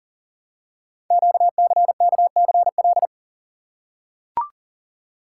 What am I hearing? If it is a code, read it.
Morse code, heard as a keyed tone, about 42 wpm, 700 Hz. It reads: QCKCL
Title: Code Groups - 5 Letters - 40wpm